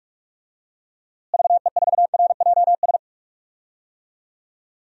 Morse code, heard as a keyed tone, about 41 wpm, 700 Hz. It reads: VE4RJS
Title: DXCC-100 Call Signs - 6 Characters - Repeated 3x - 45wpm